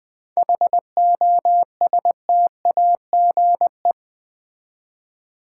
Morse code, heard as a keyed tone, about 20 wpm, 690 Hz. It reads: HOSTAGE
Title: 7 Letter Words - First 3 Letters Given - 20wpm